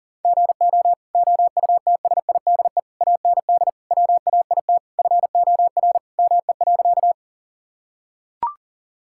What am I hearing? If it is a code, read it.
Morse code, heard as a keyed tone, about 40 wpm, 700 Hz. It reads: GO OUTSIDE AND WAIT FOR ME.
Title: Sentences from Top 1000 Words - Review Part 6 - 40wpm